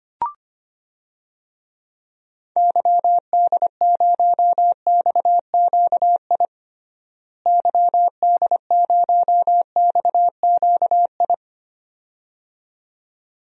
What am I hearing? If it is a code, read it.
Morse code, heard as a keyed tone, about 25 wpm, 700 Hz. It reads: YD0XQI YD0XQI
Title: DXCC-100 Call Signs - 6 Characters - Repeated 2x - 25wpm